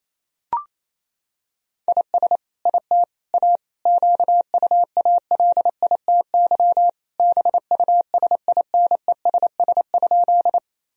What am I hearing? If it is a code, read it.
Morse code, heard as a keyed tone, about 28 wpm, 700 Hz. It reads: IS IT A QUALITY BUSINESS?
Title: Sentences from Top 1000 Words - Review Part 4 - 28wpm